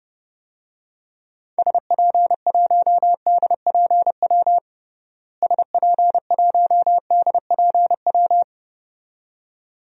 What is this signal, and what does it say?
Morse code, heard as a keyed tone, about 30 wpm, 700 Hz. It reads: SP1DPW SP1DPW
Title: DXCC-100 Call Signs - 6 Characters - Repeated 2x - 30wpm